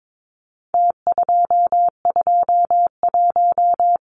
Morse code, heard as a keyed tone, about 22 wpm, 700 Hz. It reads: T221
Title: Numbers - Contest Style - 4-Digit 22wpm